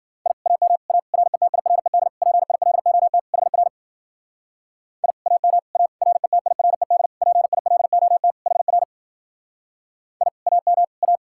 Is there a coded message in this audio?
Yes: Morse code from 0.3 to 11.2 s, heard as a keyed tone, about 52 wpm, 700 Hz. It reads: I AM A RETIRED PILOT HR I AM A RETIRED PILOT HR I AM A